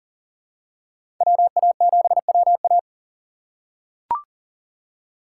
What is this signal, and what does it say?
Morse code, heard as a keyed tone, about 40 wpm, 700 Hz. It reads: WA7WA
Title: USA Call Signs - 5 Characters - 4x Character Spacing - 40wpm